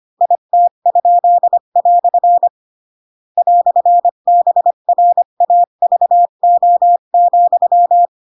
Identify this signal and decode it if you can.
Morse code, heard as a keyed tone, about 25 wpm, 700 Hz. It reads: IT?" "BRAVO,